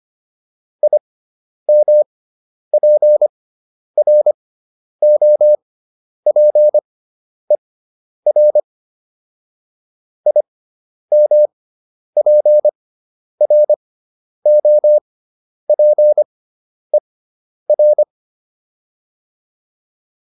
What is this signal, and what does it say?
Morse code, heard as a keyed tone, about 25 wpm, spaced as for 10 wpm, 600 Hz. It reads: IMPROPER IMPROPER